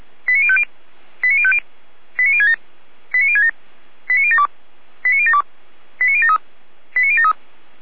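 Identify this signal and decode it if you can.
Five-tone selective call (ZVEI), heard as quick runs of five tones, tones of about 70 ms. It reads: 89050 89050 89067 89067 89072 89072 89073 89073
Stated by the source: ZVEI five-tone selective call 89001 received with an SDR.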